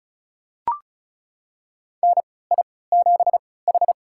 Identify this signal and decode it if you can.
Morse code, heard as a keyed tone, about 34 wpm, spaced as for 20 wpm, 700 Hz. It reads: NI7H
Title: USA Call Signs - 4 Characters - 3x Character Spacing - 35wpm